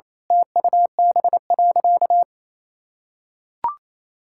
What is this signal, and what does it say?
Morse code, heard as a keyed tone, about 28 wpm, 700 Hz. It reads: TUB.
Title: Pangram Challenge v2 - 28wpm